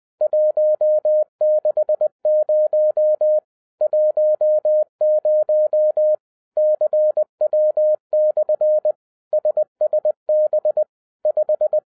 Morse code, heard as a keyed tone, about 20 wpm, 610 Hz. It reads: 160 10 CW/ SSB 5
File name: 2011-07-mp3 / n23 QRP OTAKU 20wpm